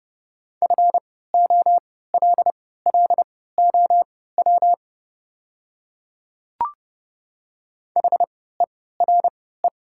Morse code, heard as a keyed tone, about 30 wpm, spaced as for 17 wpm, 700 Hz. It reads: FOLLOW HERE
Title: 3x Character Spacing - Top 200 Words - 30wpm